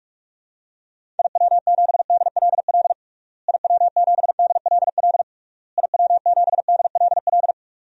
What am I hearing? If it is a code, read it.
Morse code, heard as a keyed tone, about 41 wpm, 700 Hz. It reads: IW7DLL IW7DLL IW7DLL
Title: DXCC-100 Call Signs - 6 Characters - Repeated 3x - 45wpm